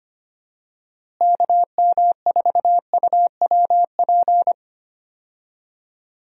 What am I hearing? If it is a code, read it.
Morse code, heard as a keyed tone, about 25 wpm, 700 Hz. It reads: KM4UWP
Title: USA Call Signs - 6 Characters - Repeated 3x - 25wpm